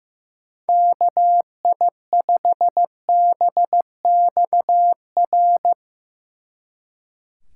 Morse code, heard as a keyed tone, about 18 wpm, 700 Hz. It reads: KI5BXR
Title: Speed Racing - USA 2x3 Call Signs 15wpm